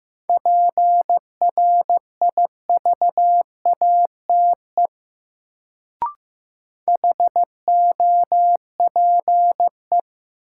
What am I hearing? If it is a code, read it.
Morse code, heard as a keyed tone, about 15 wpm, 700 Hz. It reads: PRIVATE HOPE
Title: Sentences from Top 1000 Words - New Word Drill - 15wpm